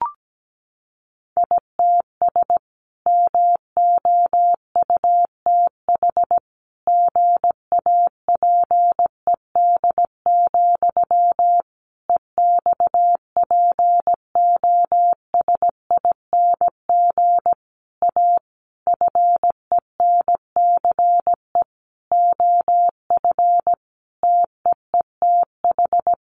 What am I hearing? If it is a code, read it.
Morse code, heard as a keyed tone, about 17 wpm, 700 Hz. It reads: ITS MOUTH GAPED, EXPOSING A FENCE OF TEETH
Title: A Sound Of Thunder 17wpm